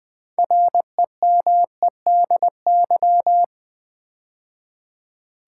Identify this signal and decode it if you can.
Morse code, heard as a keyed tone, about 20 wpm, 700 Hz. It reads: REMEDY
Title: Build A Word - 2 Letters At A Time - 6-Letter Words - 20wpm